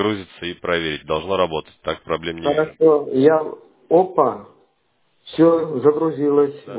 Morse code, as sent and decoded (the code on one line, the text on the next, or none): none